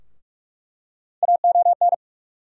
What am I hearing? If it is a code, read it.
Morse code, heard as a keyed tone, about 46 wpm, 700 Hz. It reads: AON
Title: USA Call Sign Postfixes - 3 Letters 45wpm